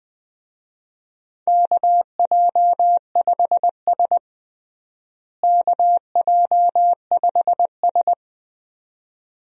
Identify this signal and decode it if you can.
Morse code, heard as a keyed tone, about 21 wpm, 700 Hz. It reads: KJ5S KJ5S
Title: USA Call Signs - 4 Characters - Repeated 2x - 20wpm